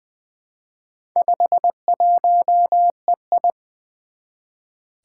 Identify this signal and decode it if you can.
Morse code, heard as a keyed tone, about 20 wpm, 700 Hz. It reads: 51EI
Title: Four Challenging Letter-Number Combinations 20wpm